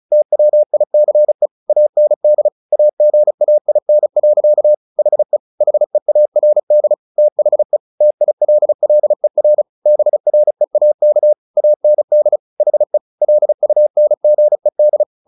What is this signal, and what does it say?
Morse code, heard as a keyed tone, about 34 wpm, 600 Hz. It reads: TWICE AND AGAIN. HE HEARD THE TILLER BREAK AND HE LUNGED
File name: the old man and the sea 2 35wpm